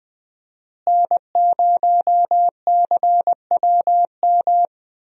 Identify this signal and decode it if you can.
Morse code, heard as a keyed tone, about 19 wpm, 700 Hz. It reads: N0CWM
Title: USA Call Signs - 5 Characters - Repeated 3x - 20wpm